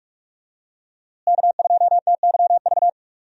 Morse code, heard as a keyed tone, about 47 wpm, 700 Hz. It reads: K2TYV
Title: USA Call Signs - 5 Characters - Repeated 2x - 45wpm